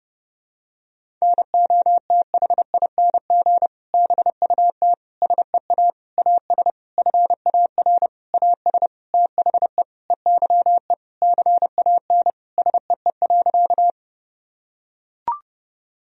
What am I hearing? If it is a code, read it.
Morse code, heard as a keyed tone, about 32 wpm, 700 Hz. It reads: NOTHING BUT SEA AS FAR AS THE EYE CAN SEE.
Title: Sentences from Top 1000 Words - Review Part 1 - 30wpm